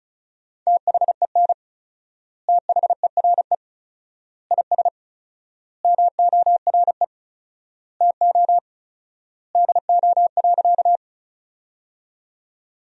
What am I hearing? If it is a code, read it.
Morse code, heard as a keyed tone, about 36 wpm, 700 Hz. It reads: THEN THERE IS MORE TO DO.
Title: 4x Word-Spacing - Sentences from Top 100 Words - 35wpm